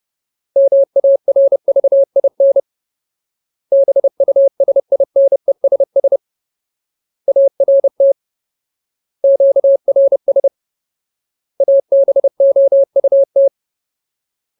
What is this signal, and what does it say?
Morse code, heard as a keyed tone, about 30 wpm, 550 Hz. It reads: MARVIN BUSINESS ART QRS ABOUT
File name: Ultimate Word List / 30WPM Ultimate